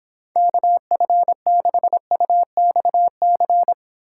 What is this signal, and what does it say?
Morse code, heard as a keyed tone, about 26 wpm, 700 Hz. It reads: KF6UXC